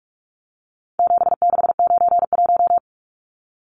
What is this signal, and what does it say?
Morse code, heard as a keyed tone, about 44 wpm, 700 Hz. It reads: 7691
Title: Numbers - 4-Digit 45wpm